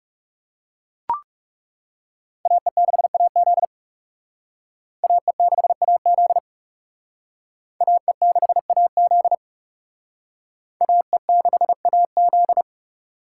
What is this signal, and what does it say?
Morse code, heard as a keyed tone, about 37 wpm, 700 Hz. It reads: AE6AZ AE6AZ AE6AZ AE6AZ
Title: Speed Racing - USA 2x2 Call Signs 30wpm